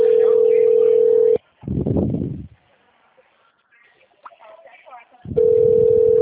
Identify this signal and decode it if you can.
Ringback tone (North America), 440 + 480 Hz.
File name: out- -2016- - - .94112